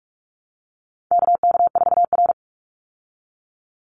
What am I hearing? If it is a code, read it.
Morse code, heard as a keyed tone, about 44 wpm, 700 Hz. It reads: KK4R
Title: Call Signs Repeated 2x - 45wpm